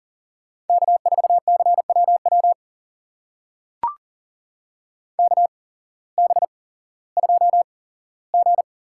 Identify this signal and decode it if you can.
Morse code, heard as a keyed tone, about 38 wpm, spaced as for 18 wpm, 700 Hz. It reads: K4CWW KB2G
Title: USA Call Signs - 8x Character Spacing - 40wpm